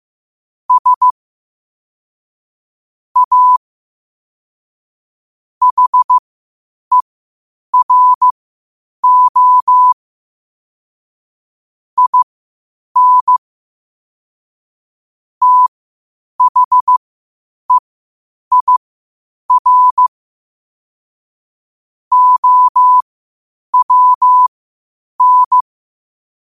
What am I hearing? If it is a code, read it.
Morse code, heard as a keyed tone, about 15 wpm, spaced as for 8 wpm, 1000 Hz. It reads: S A HERO IN THEIR OWN
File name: qotdfiles / 8wpm